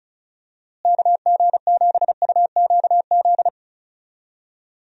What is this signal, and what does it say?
Morse code, heard as a keyed tone, about 34 wpm, 700 Hz. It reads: KG7UQZ
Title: USA Call Signs - 6 Characters - Repeated 2x - 35wpm